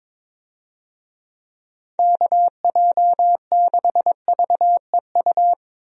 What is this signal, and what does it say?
Morse code, heard as a keyed tone, about 22 wpm, 700 Hz. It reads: KJ6VEU
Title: USA Call Signs - 6 Characters - Repeated 3x - 22wpm